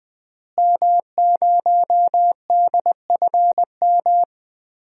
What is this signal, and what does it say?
Morse code, heard as a keyed tone, about 21 wpm, 700 Hz. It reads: M0DFM
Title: DXCC-100 Call Signs - 5 Characters - Repeated 3x - 20wpm